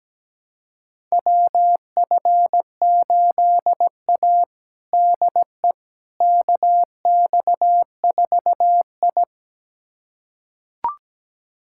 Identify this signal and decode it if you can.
Morse code, heard as a keyed tone, about 17 wpm, 700 Hz. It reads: WF8A DE KX4I
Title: USA Callsign DE Callsign - 4 Characters - 17wpm